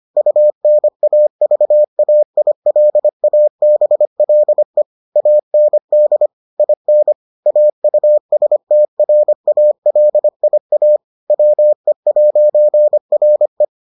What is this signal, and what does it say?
Morse code, heard as a keyed tone, about 25 wpm, 600 Hz. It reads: UNAVAILABLE AND IN AUSTRALIA WE'RE